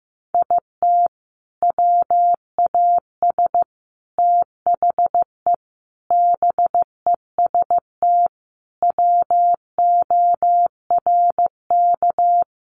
Morse code, heard as a keyed tone, about 15 wpm, 700 Hz. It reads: IT WAS THE BEST WORK
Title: Sentences from Top 1300 Words 15wpm